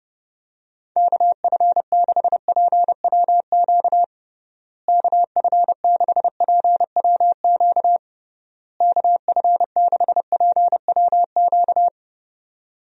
Morse code, heard as a keyed tone, about 30 wpm, 700 Hz. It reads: KF6PWQ KF6PWQ KF6PWQ
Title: USA Call Signs - 6 Characters - Repeated 3x - 30wpm